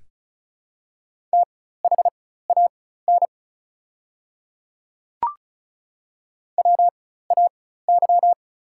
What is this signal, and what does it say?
Morse code, heard as a keyed tone, about 33 wpm, spaced as for 16 wpm, 690 Hz. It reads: THAN WAY
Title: 4x Character Spacing - Top 100 Words - 35wpm